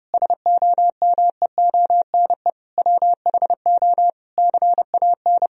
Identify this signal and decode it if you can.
Morse code, heard as a keyed tone, about 30 wpm, 700 Hz. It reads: SOMEONE WHO CAN
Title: Sentences from Top 1000 Words - Review Part 4 - 30wpm